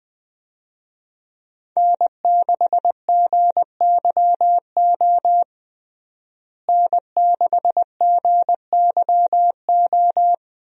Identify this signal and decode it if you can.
Morse code, heard as a keyed tone, about 21 wpm, 700 Hz. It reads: N6GYO N6GYO
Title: USA Call Signs - 5 Characters - Repeated 3x - 20wpm